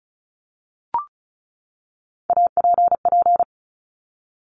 Morse code, heard as a keyed tone, about 36 wpm, 700 Hz. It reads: APP
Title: Top Three Letter Combinations in Words 35wpm